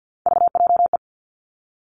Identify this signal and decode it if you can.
Morse code, heard as a keyed tone, about 50 wpm, 700 Hz. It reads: VJE
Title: Three Challenging Letter-Number Combinations 50wpm